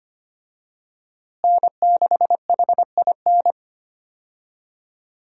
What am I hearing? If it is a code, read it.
Morse code, heard as a keyed tone, about 25 wpm, 700 Hz. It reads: N6HIN
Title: USA Call Signs - 5 Characters - Repeated 2x - 25wpm